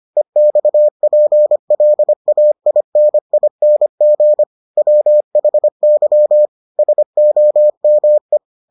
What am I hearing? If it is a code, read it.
Morse code, heard as a keyed tone, about 25 wpm, 600 Hz. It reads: EXPLAINING WHY SOME